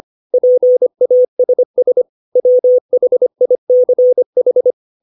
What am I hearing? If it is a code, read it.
Morse code, heard as a keyed tone, about 25 wpm, 500 Hz. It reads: PASS WHICH
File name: 632 words / 632 Words 25WPM